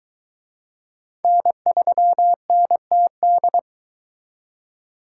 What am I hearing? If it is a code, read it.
Morse code, heard as a keyed tone, about 22 wpm, 700 Hz. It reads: N3NTD